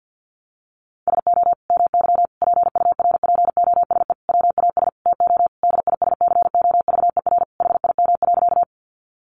Morse code, heard as a keyed tone, about 49 wpm, 700 Hz. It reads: SO MY PURPOSE WAS TO DISCOVER HIM.
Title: Sentences from Top 1100 Words 50wpm